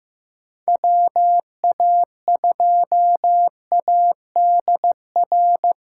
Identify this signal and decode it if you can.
Morse code, heard as a keyed tone, about 15 wpm, 700 Hz. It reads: WA2ADR
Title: USA Callsign DE Callsign - 6 Characters - 15wpm